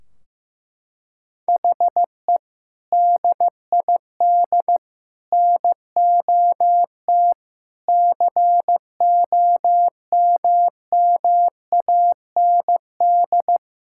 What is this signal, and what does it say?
Morse code, heard as a keyed tone, about 15 wpm, 700 Hz. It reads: HE DID NOT COMMAND
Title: Sentences from Top 1600 Words 15wpm